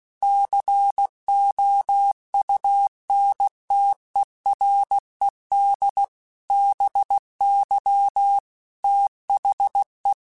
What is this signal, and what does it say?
Morse code, heard as a keyed tone, about 16 wpm, 800 Hz. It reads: COUNTERED BY THE